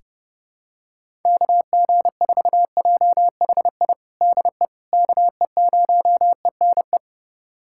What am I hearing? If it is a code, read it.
Morse code, heard as a keyed tone, about 32 wpm, 700 Hz. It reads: KG4JHI DE KE0ENE
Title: USA Callsign DE Callsign 30wpm